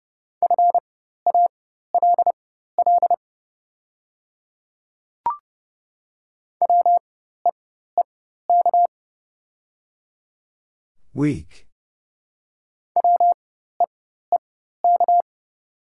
Morse code, heard as a keyed tone, about 30 wpm, spaced as for 14 wpm, 690 Hz. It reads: FALL WEEK WEEK
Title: 4x Character Spacing - Top 500 Words - 30wpm